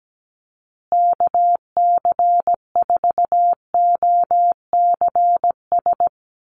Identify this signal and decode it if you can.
Morse code, heard as a keyed tone, about 17 wpm, 700 Hz. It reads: KC4OCS
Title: USA Call Signs Repeated 2x 17wpm